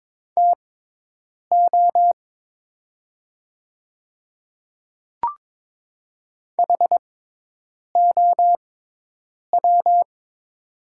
Morse code, heard as a keyed tone, about 22 wpm, spaced as for 8 wpm, 700 Hz. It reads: TO HOW
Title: 6x Character Spacing - Top 200 Words - 22wpm